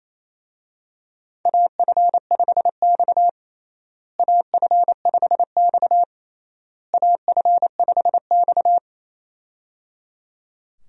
Morse code, heard as a keyed tone, about 27 wpm, 700 Hz. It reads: AF5X AF5X AF5X
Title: USA Call Signs - 4 Characters - Repeated 3x - 28wpm